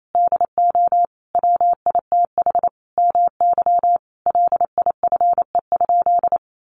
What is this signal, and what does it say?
Morse code, heard as a keyed tone, about 28 wpm, 700 Hz. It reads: DO WITH MY LIFE?
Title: Sentences from Top 300 Words 28wpm